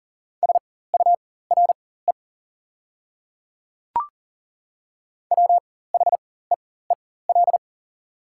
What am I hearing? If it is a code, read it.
Morse code, heard as a keyed tone, about 37 wpm, spaced as for 18 wpm, 700 Hz. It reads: SURE WHEEL
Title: 4x Character Spacing - Top 500 Words - 40wpm